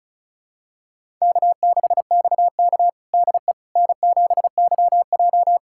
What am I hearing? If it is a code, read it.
Morse code, heard as a keyed tone, about 36 wpm, 700 Hz. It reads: K6XK DE N7YJ
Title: USA Callsign DE Callsign - 4 Characters - 35wpm